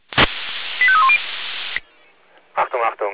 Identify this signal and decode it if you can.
Five-tone selective call (ZVEI), heard as quick runs of five tones, tones of about 70 ms, with speech. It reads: 85310